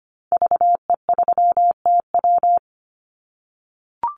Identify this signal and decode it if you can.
Morse code, heard as a keyed tone, about 25 wpm, 700 Hz. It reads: VE3TW
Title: Call Signs - 25wpm